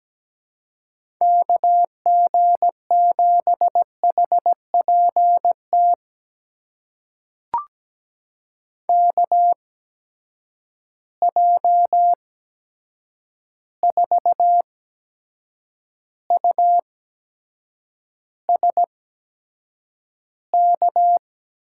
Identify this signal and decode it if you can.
Morse code, heard as a keyed tone, about 17 wpm, spaced as for 7 wpm, 700 Hz. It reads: KG7HPT KJ4USK
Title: USA Call Signs - 6 Characters - 8x Character Spacing - 17wpm